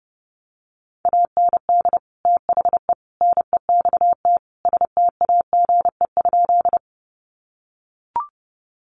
Morse code, heard as a keyed tone, about 28 wpm, 700 Hz. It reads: AND THE NEXT STAGE?
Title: Sentences from Top 1200 Words 30wpm